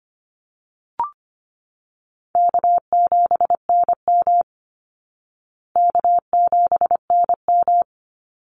Morse code, heard as a keyed tone, about 25 wpm, 700 Hz. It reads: K7NM K7NM
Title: Call Signs Repeated 2x - 25wpm